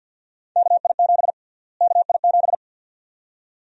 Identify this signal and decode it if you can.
Morse code, heard as a keyed tone, about 52 wpm, 700 Hz. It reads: KI7 KI7
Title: USA Call Sign Prefixes - Repeated 2x - 50wpm